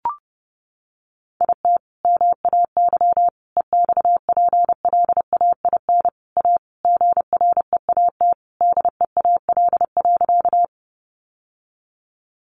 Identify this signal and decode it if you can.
Morse code, heard as a keyed tone, about 30 wpm, 700 Hz. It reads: IT MAY EXPLAIN A GREAT DEAL.
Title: Sentences from Top 1100 Words 30wpm